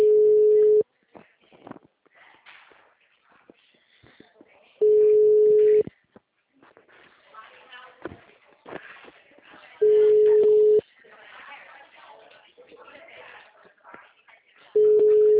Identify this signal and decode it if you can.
Ringback tone (Europe, 425 Hz), with speech.
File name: out- -2020- - - .77